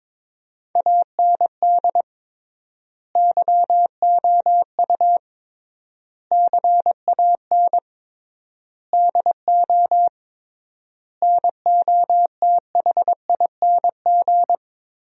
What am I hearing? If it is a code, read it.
Morse code, heard as a keyed tone, about 22 wpm, 700 Hz. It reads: AND YOU CAN DO NOTHING
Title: 3x Word-Spacing - Sentences from Top 500 Words - 22wpm